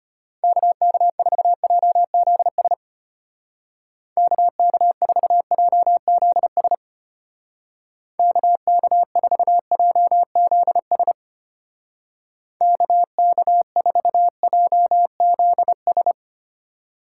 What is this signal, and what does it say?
Morse code, heard as a keyed tone, about 33 wpm, 700 Hz. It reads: KK4JZS KK4JZS KK4JZS KK4JZS